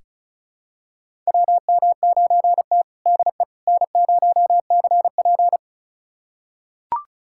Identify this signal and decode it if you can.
Morse code, heard as a keyed tone, about 36 wpm, 700 Hz. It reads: WM9T DE N0CP
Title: USA Callsign DE Callsign - 4 Characters - 35wpm